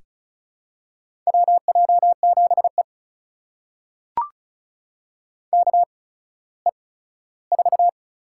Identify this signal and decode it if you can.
Morse code, heard as a keyed tone, about 36 wpm, spaced as for 17 wpm, 700 Hz. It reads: WJ7E KE4
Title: USA Call Signs - 8x Character Spacing - 35wpm